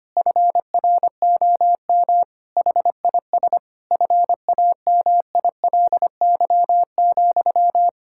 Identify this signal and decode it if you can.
Morse code, heard as a keyed tone, about 25 wpm, 700 Hz. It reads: FROM HIS FAMILY,
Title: Sentences from Top 1000 Words - Review Part 6 - 25wpm